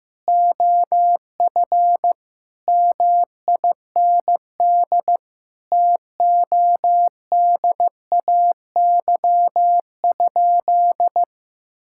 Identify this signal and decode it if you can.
Morse code, heard as a keyed tone, about 16 wpm, 700 Hz. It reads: OF MIND TODAY?
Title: Sentences from Top 1000 Words - Review Part 6 - 15wpm